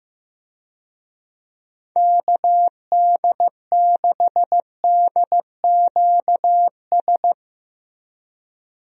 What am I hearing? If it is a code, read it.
Morse code, heard as a keyed tone, about 15 wpm, 700 Hz. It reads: KD6DQS
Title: USA Call Signs - 6 Characters - Repeated 2x - 15wpm